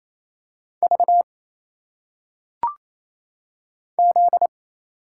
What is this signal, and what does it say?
Morse code, heard as a keyed tone, about 28 wpm, 700 Hz. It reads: V Z
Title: Single Letters - Phonetic Pronunciation 28wpm